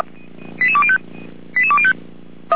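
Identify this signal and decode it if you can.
Five-tone selective call (ZVEI), heard as quick runs of five tones, tones of about 70 ms. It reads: 88205 88205
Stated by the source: ZVEI five-tone call 88124 received by SDR, logged by feuerwehr-orlamuende.org fire brigade server.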